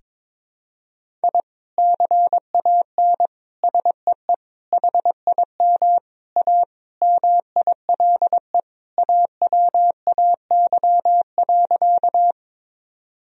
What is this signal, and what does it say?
Morse code, heard as a keyed tone, about 22 wpm, 700 Hz. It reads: I CAN SEE HIM A MILE AWAY.
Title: Sentences from Top 1000 Words - Review Part 1 - 22wpm